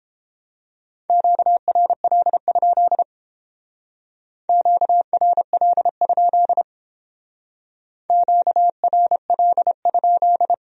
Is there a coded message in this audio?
Yes: Morse code from 1.1 to 10.5 s, heard as a keyed tone, about 31 wpm, 700 Hz. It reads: QRL? QRL? QRL?